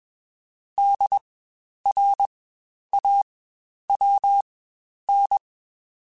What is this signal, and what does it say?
Morse code, heard as a keyed tone, about 21 wpm, spaced as for 10 wpm, 800 Hz. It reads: DRAWN
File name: Spaced Alphas / Spaced Alphas mp3s / Ds sp @20